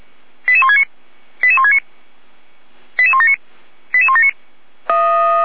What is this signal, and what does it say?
Five-tone selective call (ZVEI), heard as quick runs of five tones, tones of about 70 ms. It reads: 88169 88169 88179 88179
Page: /455/SDR/ZVEI/88124/